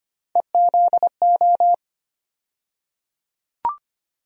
Morse code, heard as a keyed tone, about 25 wpm, 700 Hz. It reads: EZO
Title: USA Call Sign Postfixes - 3 Letters 25wpm